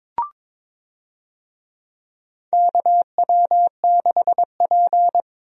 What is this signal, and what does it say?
Morse code, heard as a keyed tone, about 21 wpm, 700 Hz. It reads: KW6P
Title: USA Call Signs - 4 Characters - Repeated 2x - 22wpm